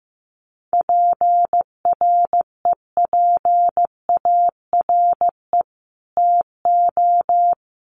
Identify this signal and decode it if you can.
Morse code, heard as a keyed tone, about 15 wpm, 700 Hz. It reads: PREPARE TO
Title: Sentences from Top 900 Words 15wpm